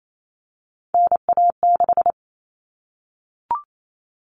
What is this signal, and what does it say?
Morse code, heard as a keyed tone, about 28 wpm, 700 Hz. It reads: NA6
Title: Common Cut Numbers - 3-Digit 28wpm